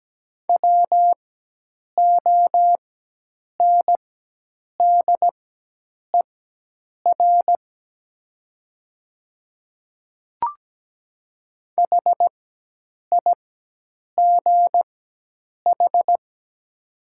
Morse code, heard as a keyed tone, about 17 wpm, spaced as for 8 wpm, 700 Hz. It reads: WONDER HIGH
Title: 4x Character Spacing - Top 500 Words - 17wpm